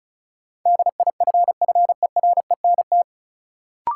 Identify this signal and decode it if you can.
Morse code, heard as a keyed tone, about 36 wpm, 700 Hz. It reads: DIFFERENT
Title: Sentences from Top 600 Words - New Word Drill - 35wpm